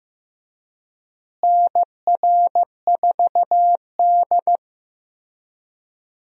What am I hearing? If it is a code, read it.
Morse code, heard as a keyed tone, about 15 wpm, 700 Hz. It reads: NR4D
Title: USA Call Signs - 4 Characters - Repeated 2x - 15wpm